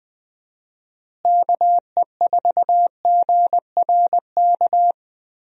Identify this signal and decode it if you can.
Morse code, heard as a keyed tone, about 21 wpm, 700 Hz. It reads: KE4GRK
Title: USA Call Signs - 6 Characters - Repeated 2x - 20wpm